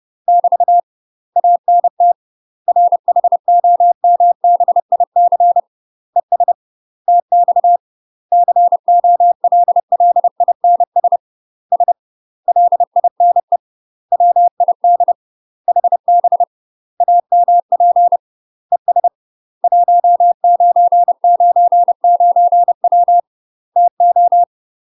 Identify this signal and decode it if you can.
Morse code, heard as a keyed tone, about 30 wpm, 700 Hz. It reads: <BT> ANT RHOMBIC ES TX COLLINS S LINE WID HB AMP ES 1999W TO
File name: UK MASTER QSOs TRAINING / 30WPM / UK MASTER QSOs 0000